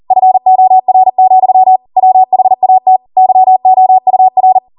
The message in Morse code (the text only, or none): FOR, WHAT YOUR